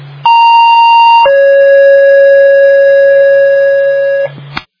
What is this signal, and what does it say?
Two-tone page, heard as a one-second tone, then a three-second tone, about 930 then 550 Hz. It is heard from 0.2 to 4.2 s.